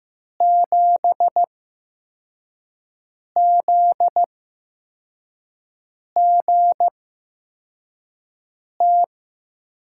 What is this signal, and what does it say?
Morse code, heard as a keyed tone, about 15 wpm, spaced as for 4 wpm, 700 Hz. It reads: 7ZGT
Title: USA Call Signs - 8x Character Spacing - 15wpm